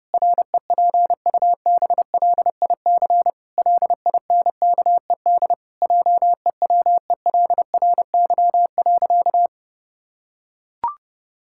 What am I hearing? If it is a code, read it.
Morse code, heard as a keyed tone, about 30 wpm, 700 Hz. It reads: REPUBLIC LINKED JEWELRY.
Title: Pangram Challenge v2 - 30wpm